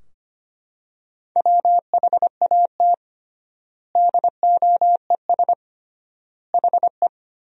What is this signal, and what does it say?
Morse code, heard as a keyed tone, about 25 wpm, 700 Hz. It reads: WHAT DOES HE
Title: 3x Word-Spacing - Sentences from Top 400 Words - 25wpm